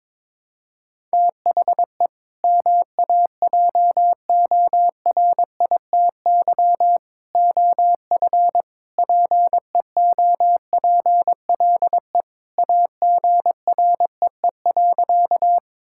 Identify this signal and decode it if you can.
Morse code, heard as a keyed tone, about 21 wpm, 700 Hz. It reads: THE MAJORITY OF PEOPLE AGREE.
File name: practice / 22wpm / Sentences with 3-Word Phrases - 22wpm